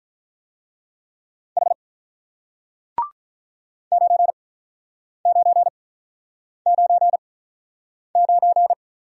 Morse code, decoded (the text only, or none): H 9999